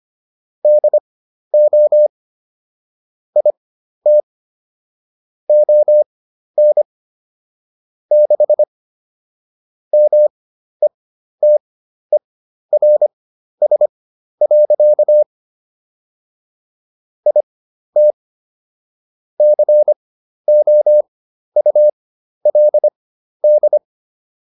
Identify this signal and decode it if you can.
Morse code, heard as a keyed tone, about 25 wpm, spaced as for 12 wpm, 600 Hz. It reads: DO IT ON 6 METERS. IT COULD